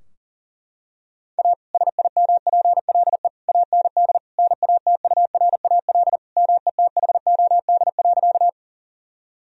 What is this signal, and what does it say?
Morse code, heard as a keyed tone, about 40 wpm, 700 Hz. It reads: A SIMPLE AND NATURAL METHOD.
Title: Sentences from Top 1000 Words - Review Part 4 - 40wpm